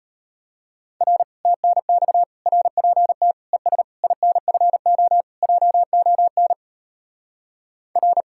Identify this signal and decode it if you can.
Morse code, heard as a keyed tone, about 32 wpm, 700 Hz. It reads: R TNX RPT ES INFO JON R